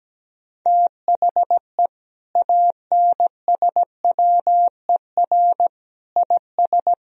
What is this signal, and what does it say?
Morse code, heard as a keyed tone, about 17 wpm, 700 Hz. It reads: THE ANSWER IS
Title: Sentences from Top 1000 Words - Review Part 2 - 17wpm